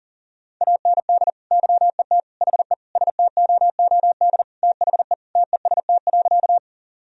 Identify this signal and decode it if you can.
Morse code, heard as a keyed tone, about 40 wpm, 700 Hz. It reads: AND YET HE STOOD THE TEST.
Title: Sentences from Top 1000 Words - Review Part 3 - 40wpm